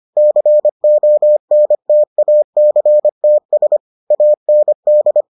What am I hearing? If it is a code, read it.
Morse code, heard as a keyed tone, about 25 wpm, 600 Hz. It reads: CONTACTS AND